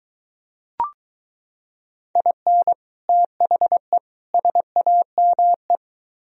Morse code, heard as a keyed tone, about 23 wpm, 700 Hz. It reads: IN THE SAME